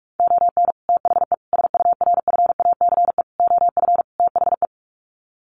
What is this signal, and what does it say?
Morse code, heard as a keyed tone, about 44 wpm, 700 Hz. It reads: ON THE SURFACE OF THE
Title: Sets of 5 Words Encore 45wpm